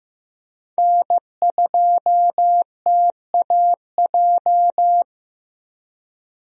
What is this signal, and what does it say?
Morse code, heard as a keyed tone, about 15 wpm, 700 Hz. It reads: N2TAJ
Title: USA Call Signs - 5 Characters - Repeated 2x - 15wpm